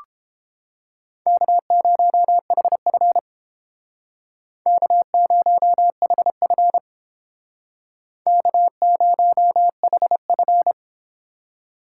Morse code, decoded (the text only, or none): K0HF K0HF K0HF